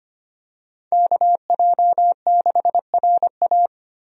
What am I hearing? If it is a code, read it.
Morse code, heard as a keyed tone, about 25 wpm, 700 Hz. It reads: KJ6RA
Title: USA Call Signs - 5 Characters - Repeated 3x - 25wpm